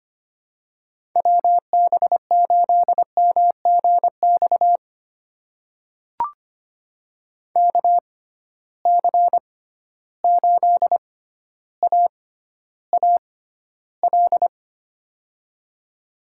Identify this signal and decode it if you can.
Morse code, heard as a keyed tone, about 25 wpm, spaced as for 13 wpm, 700 Hz. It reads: WB8MGX KC8AAL
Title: USA Call Signs - 6 Characters - 6x Character Spacing - 25wpm